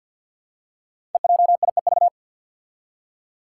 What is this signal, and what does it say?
Morse code, heard as a keyed tone, about 49 wpm, 700 Hz. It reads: EJIEV
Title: Five Challenging Letter-Number Combinations 50wpm